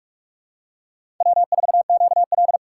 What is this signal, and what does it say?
Morse code, heard as a keyed tone, about 46 wpm, 700 Hz. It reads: W4QL
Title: USA Call Signs - 4 Characters - 4x Character Spacing - 45wpm